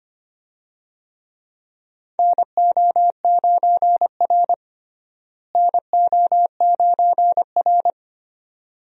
Morse code, heard as a keyed tone, about 25 wpm, 700 Hz. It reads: NO9R NO9R
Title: USA Call Signs - 4 Characters - Repeated 2x - 25wpm